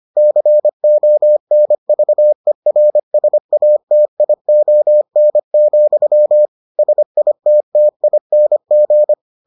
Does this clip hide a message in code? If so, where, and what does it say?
Yes: Morse code from 0.2 to 9.1 s, heard as a keyed tone, about 25 wpm, 600 Hz. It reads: CONVERSATION, SITTING